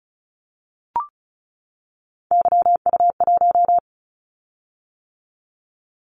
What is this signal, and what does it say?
Morse code, heard as a keyed tone, about 34 wpm, 700 Hz. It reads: YU1
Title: Callsign Prefixes for Top 25 DXCC - 35wpm